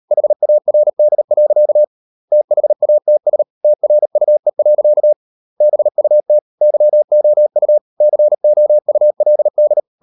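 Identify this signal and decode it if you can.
Morse code, heard as a keyed tone, about 38 wpm, 600 Hz. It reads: HARD. THATS TRUE. BUT YOU COULD
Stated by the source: Morse code practice at 38 WPM, sending Ken Follett's novel Never in English, from CW med Gnister.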